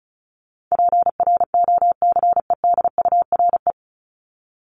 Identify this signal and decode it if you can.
Morse code, heard as a keyed tone, about 34 wpm, 700 Hz. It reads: PROCEDURE
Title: Top 1751-2000 Words 35wpm